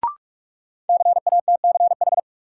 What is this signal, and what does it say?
Morse code, heard as a keyed tone, about 46 wpm, 700 Hz. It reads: CATCH
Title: Sentences from Top 1500 Words - New Word Drill - 45wpm